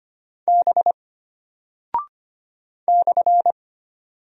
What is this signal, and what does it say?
Morse code, heard as a keyed tone, about 25 wpm, 700 Hz. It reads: B /
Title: Single Character (Letter:Number:Punctuation:Prosign) Alt 25wpm